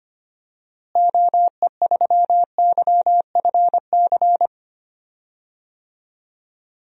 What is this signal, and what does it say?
Morse code, heard as a keyed tone, about 25 wpm, 700 Hz. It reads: OE3YFC
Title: DXCC-100 Call Signs - 6 Characters - Repeated 2x - 25wpm